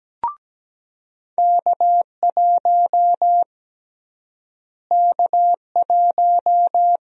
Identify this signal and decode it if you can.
Morse code, heard as a keyed tone, about 17 wpm, 700 Hz. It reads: K1 K1
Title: USA Call Sign Prefixes - Repeated 2x - 17wpm